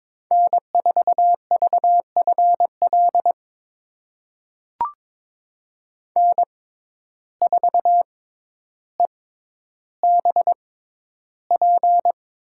Morse code, heard as a keyed tone, about 22 wpm, spaced as for 11 wpm, 700 Hz. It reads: N4VFL N4EBP
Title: USA Call Signs - 5 Characters - 6x Character Spacing - 22wpm